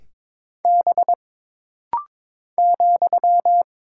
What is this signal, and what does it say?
Morse code, heard as a keyed tone, about 22 wpm, 700 Hz. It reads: B ,
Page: This rapid-fire timing is geared to push your ICR (Instant Character Recognition) skills.